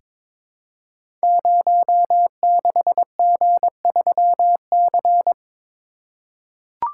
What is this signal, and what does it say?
Morse code, heard as a keyed tone, about 22 wpm, 700 Hz. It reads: 06G3C
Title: Code Groups - 5 Letter:Numbers - 22wpm